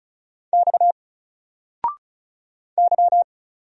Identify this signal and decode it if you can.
Morse code, heard as a keyed tone, about 34 wpm, 700 Hz. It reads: X Y
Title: Single Letter-Number 35wpm